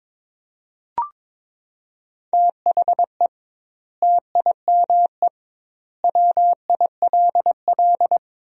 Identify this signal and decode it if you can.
Morse code, heard as a keyed tone, about 22 wpm, 700 Hz. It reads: THE TIME WILL
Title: 2x Word-Spacing - Sentences from Top 100 Words - 22wpm